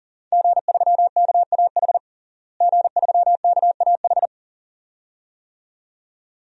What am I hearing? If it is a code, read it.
Morse code, heard as a keyed tone, about 40 wpm, 700 Hz. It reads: G3KAH G3KAH
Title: DXCC-100 Call Signs - 5 Characters - Repeated 2x - 40wpm